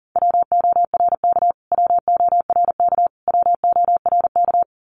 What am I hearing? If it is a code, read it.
Morse code, heard as a keyed tone, about 40 wpm, 700 Hz. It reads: WORK WORK WORK
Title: Top 500 Words Repeated 40wpm